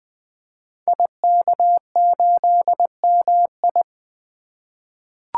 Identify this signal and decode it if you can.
Morse code, heard as a keyed tone, about 20 wpm, 700 Hz. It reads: IK8MI
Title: Code Groups - 5 Letter:Numbers - 20wpm